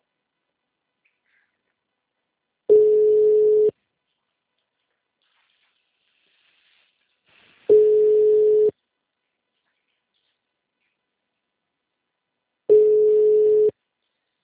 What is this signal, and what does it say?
Ringback tone (Europe, 425 Hz).